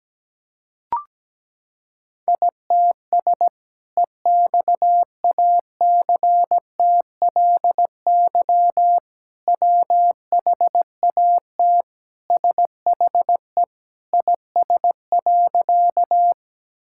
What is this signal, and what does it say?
Morse code, heard as a keyed tone, about 17 wpm, 700 Hz. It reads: ITS EXACTLY WHAT SHE IS.
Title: Sentences from Top 1000 Words - Review Part 6 - 17wpm